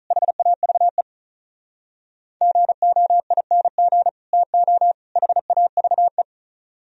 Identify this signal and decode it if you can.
Morse code, heard as a keyed tone, about 43 wpm, 700 Hz. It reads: HAVE GOING TO HAVE
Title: Speed-Racing - Sets of 3 Words 35wpm